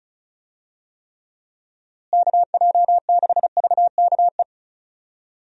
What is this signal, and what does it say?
Morse code, heard as a keyed tone, about 34 wpm, 700 Hz. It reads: KJ6VKE
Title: USA Call Signs - 6 Characters - Repeated 3x - 35wpm